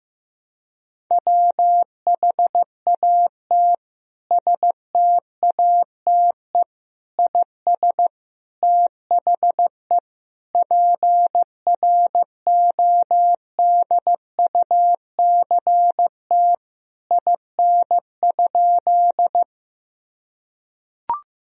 Morse code, heard as a keyed tone, about 15 wpm, 700 Hz. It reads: WHAT STATE IS THE PRODUCT IN?
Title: Sentences from Top 1000 Words - Review Part 2 - 15wpm